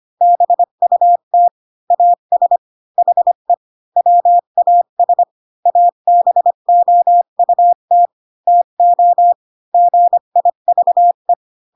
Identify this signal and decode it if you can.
Morse code, heard as a keyed tone, about 25 wpm, 700 Hz. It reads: BUT AS HE WAS ABOUT TO GIVE
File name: Pinocchio / Pinocchio ch1 20 25 ik1hge